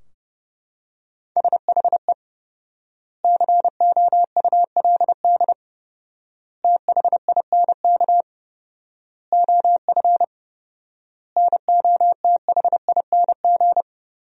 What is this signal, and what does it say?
Morse code, heard as a keyed tone, about 32 wpm, 700 Hz. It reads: SHE COULD THINK OF NOTHING
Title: 4x Word-Spacing - Sentences from Top 500 Words - 30wpm